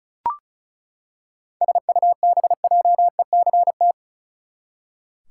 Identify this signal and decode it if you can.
Morse code, heard as a keyed tone, about 36 wpm, 700 Hz. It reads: SUBJECT
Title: Seven Letter Words - 35wpm